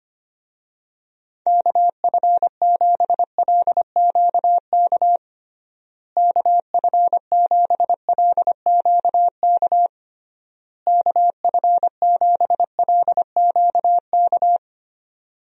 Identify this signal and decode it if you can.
Morse code, heard as a keyed tone, about 25 wpm, 700 Hz. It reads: KF7LQK KF7LQK KF7LQK
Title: USA Call Signs - 6 Characters - Repeated 3x - 25wpm